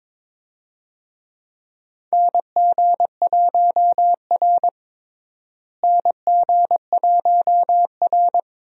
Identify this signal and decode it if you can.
Morse code, heard as a keyed tone, about 22 wpm, 700 Hz. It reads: NG1R NG1R
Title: USA Call Signs - 4 Characters - Repeated 2x - 22wpm